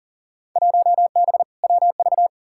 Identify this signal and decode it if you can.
Morse code, heard as a keyed tone, about 40 wpm, 700 Hz. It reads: 1B WV
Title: Field Day Exchanges - 40wpm